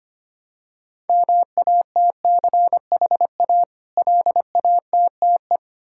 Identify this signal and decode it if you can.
Morse code, heard as a keyed tone, about 25 wpm, 700 Hz. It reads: MATCHA LATTE
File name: practice / 25wpm / Difficult Words with Hints - 25wpm